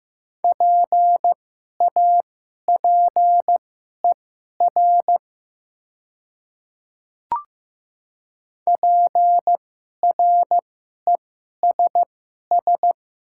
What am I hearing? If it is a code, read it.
Morse code, heard as a keyed tone, about 15 wpm, spaced as for 11 wpm, 700 Hz. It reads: PAPER PRESS
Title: 2x Character Spacing - Top 300 Words - 15wpm